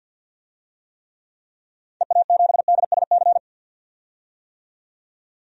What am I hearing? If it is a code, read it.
Morse code, heard as a keyed tone, about 51 wpm, 700 Hz. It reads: EA7DSC
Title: DXCC-100 Call Signs - 6 Characters - 50wpm